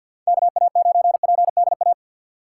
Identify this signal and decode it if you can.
Morse code, heard as a keyed tone, about 47 wpm, 700 Hz. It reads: KA9PDA
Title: USA Call Signs - 6 Characters - 6x Character Spacing - 50wpm